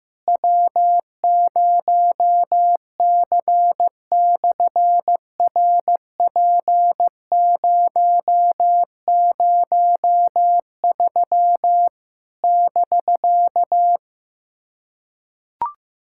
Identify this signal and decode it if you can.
Morse code, heard as a keyed tone, about 15 wpm, 700 Hz. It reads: W0C/RP003 <BK>
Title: SOTA QSO Practice 15wpm